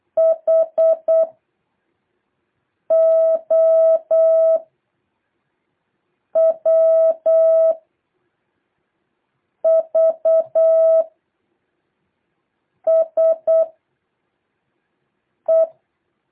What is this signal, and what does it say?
Morse code, heard as a keyed tone, about 8 wpm, 650 Hz. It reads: H O W V S E
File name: jh8cbh / 2023cw